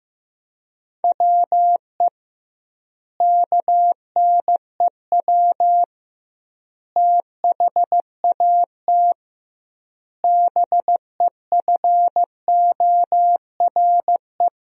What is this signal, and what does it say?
Morse code, heard as a keyed tone, about 15 wpm, 700 Hz. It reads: WE KNEW THAT BEFORE
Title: 2x Word-Spacing - Sentences from Top 400 Words - 15wpm